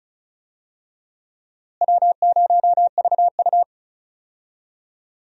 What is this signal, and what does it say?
Morse code, heard as a keyed tone, about 36 wpm, 700 Hz. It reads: W0VU
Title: USA Call Signs - 4 Characters - Repeated 2x - 35wpm